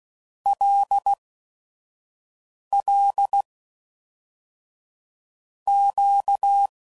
Morse code, heard as a keyed tone, about 16 wpm, 800 Hz. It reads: L L Q
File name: LFCYQ@5 f15